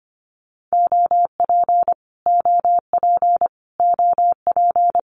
Morse code, heard as a keyed tone, about 25 wpm, 700 Hz. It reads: OP OP OP
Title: Abbreviations Repeated 3x - 25wpm